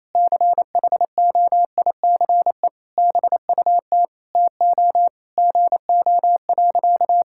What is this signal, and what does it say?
Morse code, heard as a keyed tone, about 28 wpm, 700 Hz. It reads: CHOICE BUT TO GO.
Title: Sentences from Top 1000 Words - Review Part 6 - 28wpm